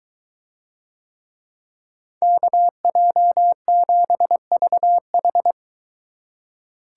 Morse code, heard as a keyed tone, about 23 wpm, 700 Hz. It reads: KJ7VH